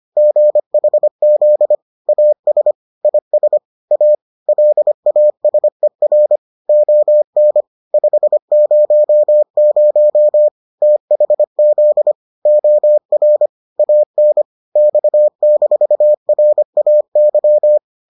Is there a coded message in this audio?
Yes: Morse code from 0.2 to 17.8 s, heard as a keyed tone, about 25 wpm, 600 Hz. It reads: GHZ AS IS A LASER ON 500 THZ OR AN X-RAY